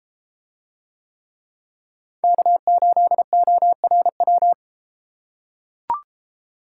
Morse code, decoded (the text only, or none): K8ORW